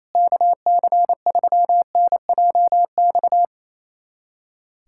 Morse code, heard as a keyed tone, about 28 wpm, 700 Hz. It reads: KC3NJX
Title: USA Callsign DE Callsign - 6 Characters - 28wpm